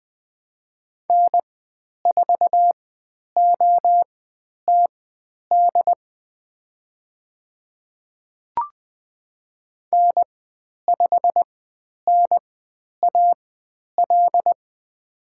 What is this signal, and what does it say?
Morse code, heard as a keyed tone, about 20 wpm, spaced as for 10 wpm, 700 Hz. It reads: N4OTD N5NAL
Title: 109 Lesson 18g - Letter m - Call Signs 20wpm